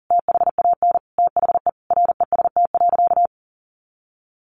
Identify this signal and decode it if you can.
Morse code, heard as a keyed tone, about 40 wpm, 700 Hz. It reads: THAN THE REST.
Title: Sentences from Top 1400 Words 40wpm